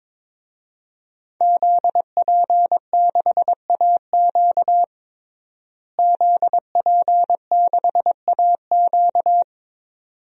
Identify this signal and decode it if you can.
Morse code, heard as a keyed tone, about 22 wpm, 700 Hz. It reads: ZP6AQ ZP6AQ
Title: DXCC-100 Call Signs - 5 Characters - Repeated 2x - 22wpm